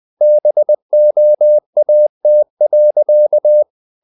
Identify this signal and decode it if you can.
Morse code, heard as a keyed tone, about 20 wpm, 600 Hz. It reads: BOAT.
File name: the old man and the sea / the old man and the sea part 1 20wpm